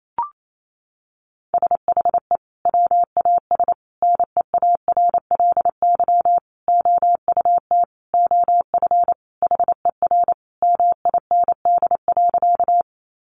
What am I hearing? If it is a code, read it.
Morse code, heard as a keyed tone, about 28 wpm, 700 Hz. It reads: SHE WAS NEARLY OUT OF HER MIND.
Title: Sentences from Top 1300 Words 28wpm